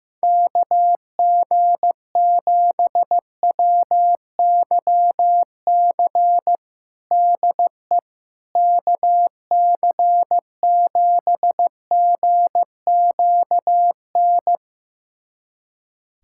Morse code, heard as a keyed tone, about 15 wpm, 700 Hz. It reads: KG7WYC DE KC7GQN
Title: USA Callsign DE Callsign 15wpm